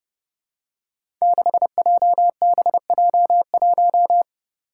Morse code, heard as a keyed tone, about 30 wpm, 700 Hz. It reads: 6JBJ1
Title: Five Challenging Letter-Number Combinations 30wpm